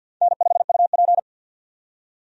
Morse code, heard as a keyed tone, about 49 wpm, 700 Hz. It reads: N5UP
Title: USA Call Signs - 4 Characters - 50wpm